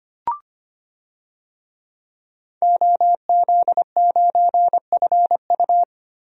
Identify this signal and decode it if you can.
Morse code, heard as a keyed tone, about 25 wpm, 700 Hz. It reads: OZ9FU
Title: DXCC-100 Call Signs - 5 Characters - 25wpm